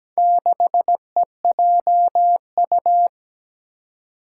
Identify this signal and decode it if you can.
Morse code, heard as a keyed tone, about 17 wpm, 700 Hz. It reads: 6EJU
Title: Four Challenging Letter-Number Combinations 17wpm